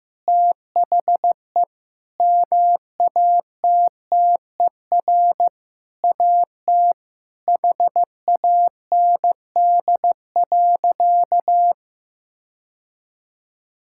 Morse code, heard as a keyed tone, about 15 wpm, 700 Hz. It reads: THE MATTER AT HAND.
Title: Sentences from Top 1600 Words 15wpm